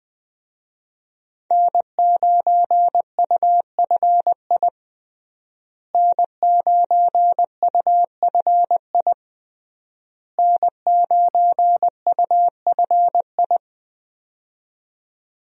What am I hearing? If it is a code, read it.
Morse code, heard as a keyed tone, about 20 wpm, 700 Hz. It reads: N9UFI N9UFI N9UFI
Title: USA Call Signs - 5 Characters - Repeated 3x - 20wpm